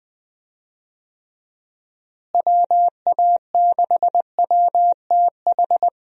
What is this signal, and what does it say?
Morse code, heard as a keyed tone, about 20 wpm, 700 Hz. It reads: WA6WTH
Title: USA Call Signs - 6 Characters - Repeated 3x - 20wpm